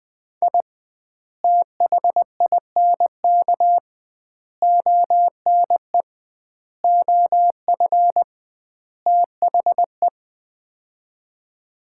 Morse code, heard as a keyed tone, about 20 wpm, 700 Hz. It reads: I THINK ONE OF THE
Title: 2x Word-Spacing - Sets of 5 Words - 20wpm